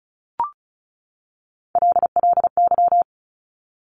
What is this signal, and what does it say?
Morse code, heard as a keyed tone, about 36 wpm, 700 Hz. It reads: LLY
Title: Top Three Letter Combinations in Words 35wpm